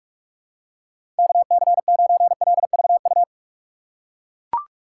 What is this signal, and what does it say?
Morse code, heard as a keyed tone, about 42 wpm, 700 Hz. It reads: KC9LVU
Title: USA Call Signs - 6 Characters - 4x Character Spacing - 45wpm